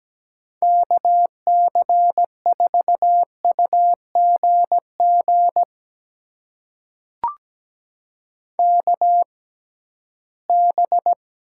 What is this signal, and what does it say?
Morse code, heard as a keyed tone, about 17 wpm, spaced as for 13 wpm, 700 Hz. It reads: KC4UGG KB
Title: USA Call Signs - 6 Characters - 6x Character Spacing - 17wpm